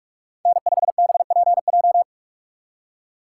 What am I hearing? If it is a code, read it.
Morse code, heard as a keyed tone, about 46 wpm, 700 Hz. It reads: N5BPJ
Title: USA Call Signs - 5 Characters - 2x Character Spacing - 45wpm